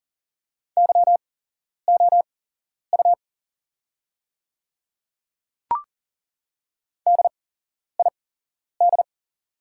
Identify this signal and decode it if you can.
Morse code, heard as a keyed tone, about 40 wpm, spaced as for 11 wpm, 700 Hz. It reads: YOU DID
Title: 8x Character Spacing - Top 100 Words - 40wpm